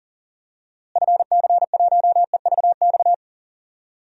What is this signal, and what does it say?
Morse code, heard as a keyed tone, about 38 wpm, 700 Hz. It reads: FC1EVX
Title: DXCC-100 Call Signs - 6 Characters - Repeated 3x - 40wpm